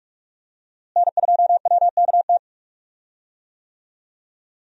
Morse code, heard as a keyed tone, about 46 wpm, 700 Hz. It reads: N2WKT
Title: USA Call Signs - 5 Characters - Repeated 3x - 45wpm